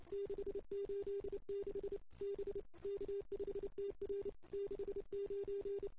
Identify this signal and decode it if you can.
Morse code, heard as a keyed tone, about 29 wpm, 400 Hz. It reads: 686 B K5TR 69
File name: CW / 2008-ss-cw-k5tr-op-n5rz -04:00